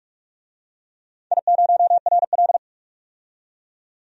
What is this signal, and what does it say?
Morse code, heard as a keyed tone, about 42 wpm, 700 Hz. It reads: I0RL
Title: DXCC-100 Call Signs - 4 Characters - 45wpm